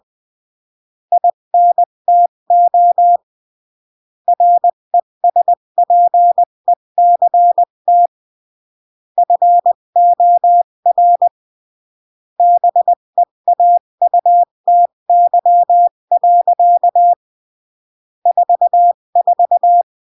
Morse code, heard as a keyed tone, about 20 wpm, 700 Hz. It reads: INTO RESPECT FOR BEAUTY. 44
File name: CW / jodorowsky / 82 commandments 20-15